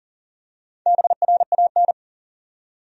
Morse code, heard as a keyed tone, about 40 wpm, 700 Hz. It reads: BRAN
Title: USA Common Names - Max 5 Letters - 40wpm